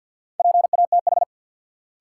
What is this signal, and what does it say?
Morse code, heard as a keyed tone, about 53 wpm, 700 Hz. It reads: PATH
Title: Sentences from Top 1400 Words - New Word Drill - 50wpm